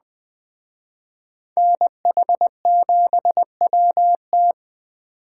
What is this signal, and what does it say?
Morse code, heard as a keyed tone, about 20 wpm, 700 Hz. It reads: NH7WT